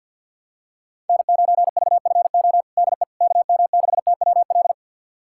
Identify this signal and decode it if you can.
Morse code, heard as a keyed tone, about 52 wpm, 690 Hz. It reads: N9VFO DE KM6TWL
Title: USA Callsign DE Callsign 50wpm